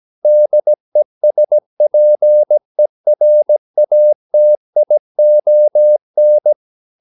Morse code, heard as a keyed tone, about 17 wpm, 600 Hz. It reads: DESPERATION
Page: Romanen Never af Ken Follet på engelsk.